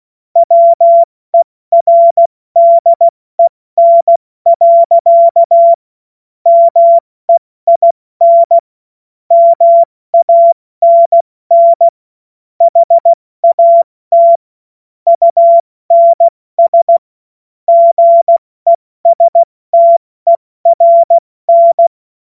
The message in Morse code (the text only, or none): WERDEN. MEIN MANN HAT UNS GESTERN